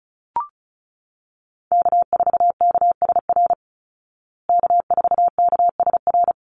Morse code, heard as a keyed tone, about 34 wpm, 700 Hz. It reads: K4KSR K4KSR
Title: Call Signs Repeated 3x - 35wpm